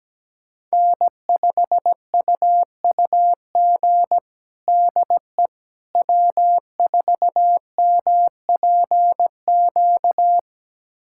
Morse code, heard as a keyed tone, about 17 wpm, 700 Hz. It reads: N5UUG DE W4MPQ
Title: USA Callsign DE Callsign - 5 Characters - 17wpm